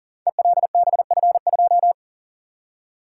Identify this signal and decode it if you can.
Morse code, heard as a keyed tone, about 40 wpm, 690 Hz. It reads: ELBF2
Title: Code Groups - 5 Letter:Numbers - 40wpm